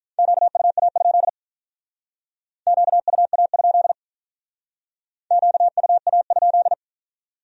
Morse code, heard as a keyed tone, about 46 wpm, 700 Hz. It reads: QUA? QUA? QUA?